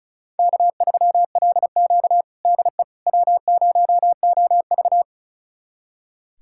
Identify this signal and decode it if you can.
Morse code, heard as a keyed tone, about 36 wpm, 700 Hz. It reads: K3LQ DE W0OV
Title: USA Callsign DE Callsign - 4 Characters - 35wpm